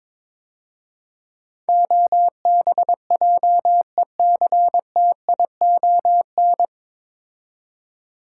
Morse code, heard as a keyed tone, about 22 wpm, 700 Hz. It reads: OBJECTION
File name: 9 Letter Words - First 3 Letters Given - 22wpm